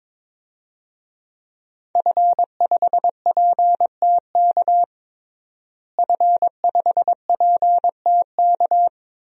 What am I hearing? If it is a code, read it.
Morse code, heard as a keyed tone, about 22 wpm, 700 Hz. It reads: F5PTK F5PTK
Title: DXCC-100 Call Signs - 5 Characters - Repeated 2x - 22wpm